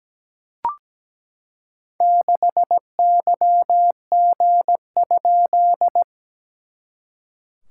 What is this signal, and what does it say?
Morse code, heard as a keyed tone, about 17 wpm, 700 Hz. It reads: 6YG?
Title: SOTA QSO Practice 17wpm